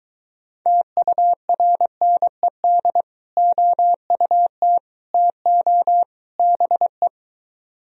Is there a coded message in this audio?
Yes: Morse code from 0.7 to 7.1 s, heard as a keyed tone, about 23 wpm, 700 Hz. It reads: TURNED OUT TO BE